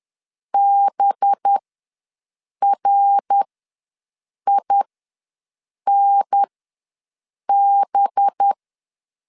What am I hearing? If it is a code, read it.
Morse code, heard as a keyed tone, about 10 wpm, spaced as for 6 wpm, 800 Hz. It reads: BRINB
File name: LEZIONI / 5 LBVC / 5 LB ws 10wpm 3616